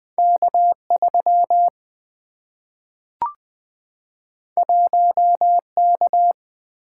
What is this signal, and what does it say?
Morse code, heard as a keyed tone, about 20 wpm, 700 Hz. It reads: K 3 1 K
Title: Single-Letter Single-Number 20wpm